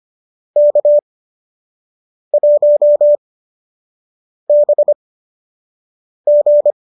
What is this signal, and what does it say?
Morse code, heard as a keyed tone, about 25 wpm, spaced as for 6 wpm, 570 Hz. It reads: K1BG